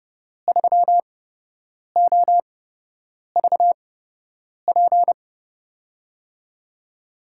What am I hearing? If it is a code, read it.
Morse code, heard as a keyed tone, about 30 wpm, spaced as for 8 wpm, 700 Hz. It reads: 3OVP
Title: USA Call Signs - 5 Characters - 8x Character Spacing - 30wpm